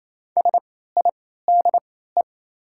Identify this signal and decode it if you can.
Morse code, heard as a keyed tone, about 27 wpm, spaced as for 16 wpm, 690 Hz. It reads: SIDE
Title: 3x Character Spacing - Top 100 Words - 28wpm